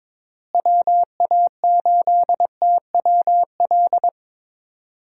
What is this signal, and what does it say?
Morse code, heard as a keyed tone, about 22 wpm, 700 Hz. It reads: WA8TWL
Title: USA Call Signs - 6 Characters - Repeated 3x - 22wpm